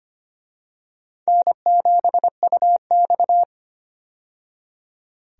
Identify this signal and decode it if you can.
Morse code, heard as a keyed tone, about 25 wpm, 700 Hz. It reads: N7UX
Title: USA Call Signs - 4 Characters - 25wpm